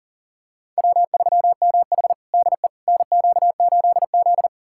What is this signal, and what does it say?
Morse code, heard as a keyed tone, about 40 wpm, 700 Hz. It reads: W3MH DE NQ8Z
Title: USA Callsign DE Callsign - 4 Characters - 40wpm